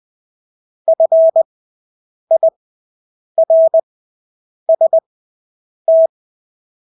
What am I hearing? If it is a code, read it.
Morse code, heard as a keyed tone, about 20 wpm, spaced as for 8 wpm, 650 Hz. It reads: FIRST
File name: Engelske ord 20/8 wpm